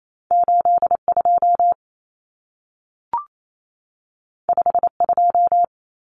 Morse code, heard as a keyed tone, about 28 wpm, 700 Hz. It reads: 82 52
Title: Common Cut Numbers - 2-Digit 28wpm